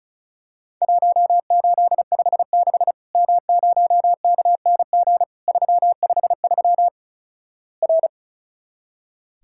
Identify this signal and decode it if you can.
Morse code, heard as a keyed tone, about 36 wpm, 700 Hz. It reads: 1856 M0KNG 353 R
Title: Worked All Europe DX Contest - QTC Exchange - 35wpm